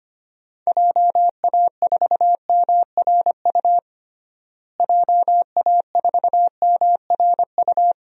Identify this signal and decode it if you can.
Morse code, heard as a keyed tone, about 25 wpm, 700 Hz. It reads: JA4MRU JA4MRU
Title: DXCC-100 Call Signs - 6 Characters - Repeated 2x - 25wpm